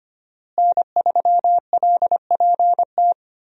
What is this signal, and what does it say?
Morse code, heard as a keyed tone, about 25 wpm, 700 Hz. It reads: N3LPT
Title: USA Call Signs - 5 Characters - Repeated 2x - 25wpm